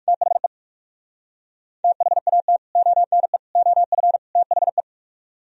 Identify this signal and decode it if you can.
Morse code, heard as a keyed tone, about 56 wpm, 700 Hz. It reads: THE THAT ONE OF THE